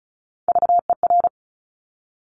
Morse code, heard as a keyed tone, about 36 wpm, 700 Hz. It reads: VER
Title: Top Three Letter Combinations in Words 35wpm